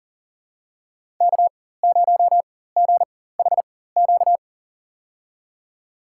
Morse code, heard as a keyed tone, about 39 wpm, spaced as for 19 wpm, 700 Hz. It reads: K0GHQ
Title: USA Call Signs - 5 Characters - 4x Character Spacing - 40wpm